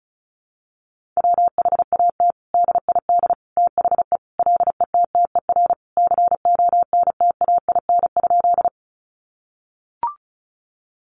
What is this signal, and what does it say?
Morse code, heard as a keyed tone, about 34 wpm, 700 Hz. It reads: WHAT DID THE LETTER CONTAIN?
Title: Sentences from Top 500 Words 35wpm